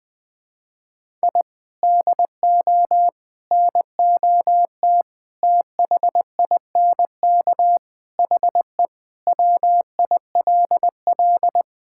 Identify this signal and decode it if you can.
Morse code, heard as a keyed tone, about 20 wpm, 700 Hz. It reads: I DO NOT THINK HE WILL
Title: Sentences with 5-Word Phrases - 20wpm